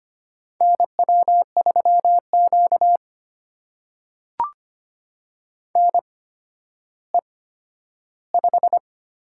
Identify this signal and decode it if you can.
Morse code, heard as a keyed tone, about 25 wpm, spaced as for 12 wpm, 700 Hz. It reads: NW3Q NE5
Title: USA Call Signs - 4 Characters - 8x Character Spacing - 25wpm